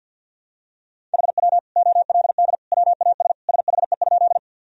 Morse code, heard as a keyed tone, about 50 wpm, 700 Hz. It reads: HW OLD WAS SHE?
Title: Sentences from Top 200 Words and Abbreviations 50wpm